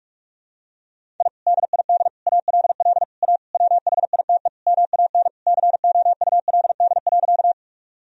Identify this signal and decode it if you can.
Morse code, heard as a keyed tone, about 46 wpm, 700 Hz. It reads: I DID ALL A WHITE MAN COULD.
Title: Sentences from Top 1000 Words - Review Part 1 - 45wpm